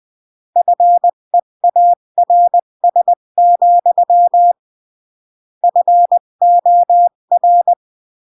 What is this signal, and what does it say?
Morse code, heard as a keyed tone, about 20 wpm, 700 Hz. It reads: FEARS, FOR
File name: CW / jodorowsky / 82 commandments 20-15